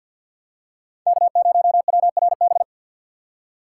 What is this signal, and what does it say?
Morse code, heard as a keyed tone, about 53 wpm, 700 Hz. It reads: K0WRB
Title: USA Call Signs - 5 Characters - 2x Character Spacing - 50wpm